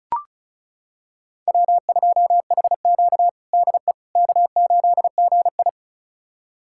Morse code, heard as a keyed tone, about 36 wpm, 700 Hz. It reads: W2HQ DE K8GI
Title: USA Callsign DE Callsign - 4 Characters - 35wpm